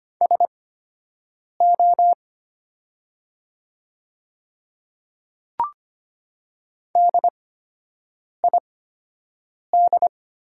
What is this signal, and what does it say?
Morse code, heard as a keyed tone, about 25 wpm, spaced as for 7 wpm, 700 Hz. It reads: SO DID
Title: 8x Character Spacing - Top 100 Words - 25wpm